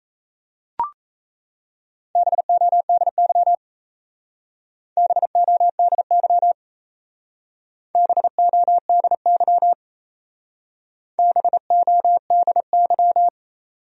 Morse code, heard as a keyed tone, about 34 wpm, 700 Hz. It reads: BODY BODY BODY BODY
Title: Speed-Racing Top 500 Words 28wpm